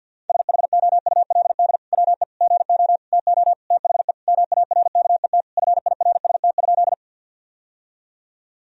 Morse code, heard as a keyed tone, about 50 wpm, 700 Hz. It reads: SHOULD WE GO TO THE MARKET FIRST?
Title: Sentences from Top 1000 Words - Review Part 3 - 50wpm